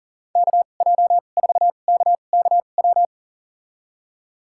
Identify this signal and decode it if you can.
Morse code, heard as a keyed tone, about 40 wpm, spaced as for 29 wpm, 700 Hz. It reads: KJ4KKW
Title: USA Call Signs - 2x Character Spacing - 40wpm